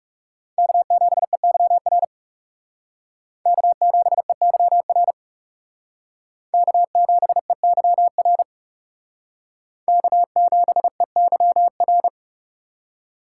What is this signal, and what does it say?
Morse code, heard as a keyed tone, about 38 wpm, 700 Hz. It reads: K7EYR K7EYR K7EYR K7EYR